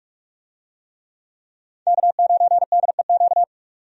Morse code, heard as a keyed tone, about 43 wpm, 700 Hz. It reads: K9DEQ
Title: USA Call Signs - 5 Characters - 45wpm